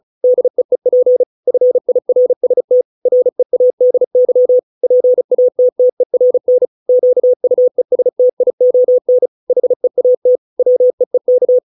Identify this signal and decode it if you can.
Morse code, heard as a keyed tone, about 34 wpm, 500 Hz. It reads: DEEP FIRST READY PATTERN QUESTION HEAT WEEK